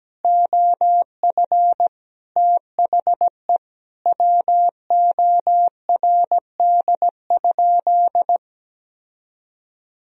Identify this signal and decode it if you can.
Morse code, heard as a keyed tone, about 17 wpm, 700 Hz. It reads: OF THE WORD?
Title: Sentences from Top 1000 Words - Review Part 4 - 17wpm